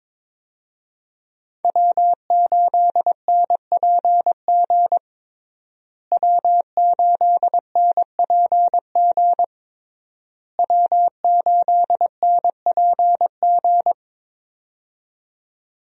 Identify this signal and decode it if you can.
Morse code, heard as a keyed tone, about 21 wpm, 700 Hz. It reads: W8NPG W8NPG W8NPG
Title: USA Call Signs - 5 Characters - Repeated 3x - 22wpm